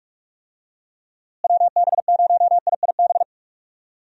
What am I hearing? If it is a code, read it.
Morse code, heard as a keyed tone, about 43 wpm, 700 Hz. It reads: WB0IIB
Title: USA Call Signs - 6 Characters - Repeated 3x - 45wpm